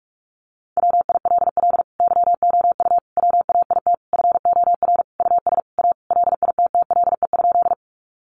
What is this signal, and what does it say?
Morse code, heard as a keyed tone, about 44 wpm, 700 Hz. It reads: WILL YOU WAIT FOR US A LITTLE?
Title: Sentences from Top 900 Words 45wpm